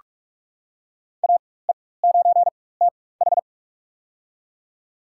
Morse code, heard as a keyed tone, about 46 wpm, spaced as for 21 wpm, 700 Hz. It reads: AE9TH
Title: USA Call Signs - 5 Characters - 4x Character Spacing - 45wpm